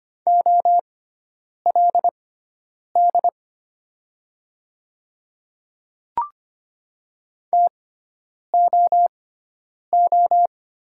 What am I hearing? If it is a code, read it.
Morse code, heard as a keyed tone, about 25 wpm, spaced as for 9 wpm, 700 Hz. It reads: OLD TOO
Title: 6x Character Spacing - Top 300 Words - 25wpm